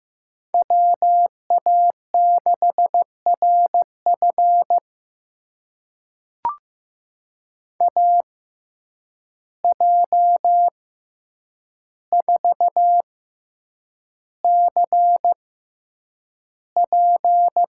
Morse code, heard as a keyed tone, about 15 wpm, spaced as for 8 wpm, 700 Hz. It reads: WA6RF AJ4CP
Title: USA Call Signs - 5 Characters - 6x Character Spacing - 15wpm